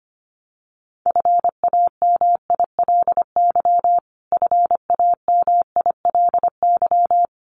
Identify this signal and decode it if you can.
Morse code, heard as a keyed tone, about 25 wpm, 700 Hz. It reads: FAMILY FAMILY
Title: Top 500 Words Repeated 25wpm